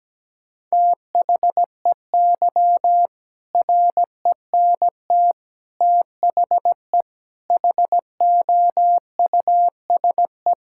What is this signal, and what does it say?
Morse code, heard as a keyed tone, about 17 wpm, 700 Hz. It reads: THEY RENT THE HOUSE
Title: Sentences from Top 1000 Words - Review Part 5 - 17wpm